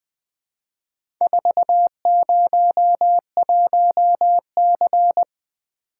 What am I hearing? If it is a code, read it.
Morse code, heard as a keyed tone, about 20 wpm, 700 Hz. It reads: 401C
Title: Four Letter-Number Combinations - 20wpm